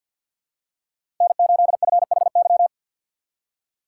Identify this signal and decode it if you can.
Morse code, heard as a keyed tone, about 48 wpm, 700 Hz. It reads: N8FHY
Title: USA Call Signs - 5 Characters - 2x Character Spacing - 50wpm